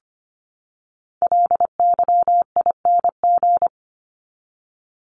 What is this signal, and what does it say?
Morse code, heard as a keyed tone, about 25 wpm, 700 Hz. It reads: LYING
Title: ING Suffix - Top 100 Words with Base Words 25wpm